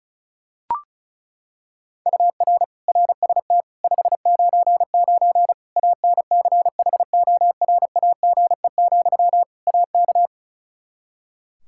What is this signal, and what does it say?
Morse code, heard as a keyed tone, about 36 wpm, 700 Hz. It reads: UR RST 599 ANCHORAGE, AK
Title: K1USN SST QSO Practice 35wpm